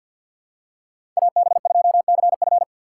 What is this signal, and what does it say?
Morse code, heard as a keyed tone, about 53 wpm, 690 Hz. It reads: AB2CF
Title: USA Call Signs - 5 Characters - 4x Character Spacing - 50wpm